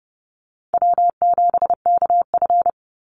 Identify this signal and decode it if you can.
Morse code, heard as a keyed tone, about 30 wpm, 700 Hz. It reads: W7KF
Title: Easy Call Signs Repeated 3x - 30wpm